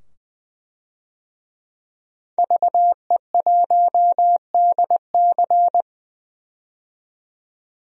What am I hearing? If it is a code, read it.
Morse code, heard as a keyed tone, about 20 wpm, 700 Hz. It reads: VE1DC
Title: DXCC-100 Call Signs - 5 Characters - 20wpm